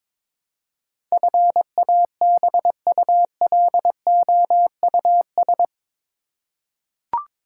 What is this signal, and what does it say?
Morse code, heard as a keyed tone, about 21 wpm, 700 Hz. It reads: FABULOUS
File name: practice / 22wpm / Eight Letter Words - 22wpm